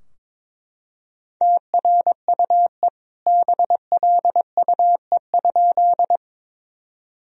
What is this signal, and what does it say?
Morse code, heard as a keyed tone, about 21 wpm, 700 Hz. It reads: TRUE BLUE?
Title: Sentences from Top 1000 Words - Review Part 3 - 22wpm